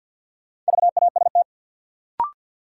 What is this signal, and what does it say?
Morse code, heard as a keyed tone, about 49 wpm, 700 Hz. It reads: VAST
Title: Sentences from Top 1300 Words - New Word Drill - 50wpm